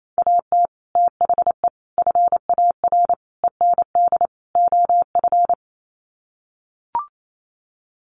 Morse code, heard as a keyed tone, about 28 wpm, 700 Hz. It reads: AT THE FAR END OF
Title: Sets of 5 Words 28wpm